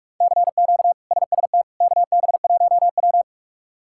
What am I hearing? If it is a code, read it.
Morse code, heard as a keyed tone, about 42 wpm, 700 Hz. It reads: CQ SST KB1W
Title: K1USN SST QSO Practice - 45wpm